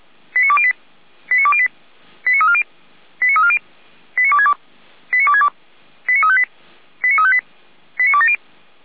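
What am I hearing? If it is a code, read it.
Five-tone selective call (ZVEI), heard as quick runs of five tones, tones of about 70 ms. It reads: 89208 89208 89340 89340 89262 89262 89358 89358 89270
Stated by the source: ZVEI five-tone call received by SDR, logged by Feuerwehr Orlamünde (fire brigade).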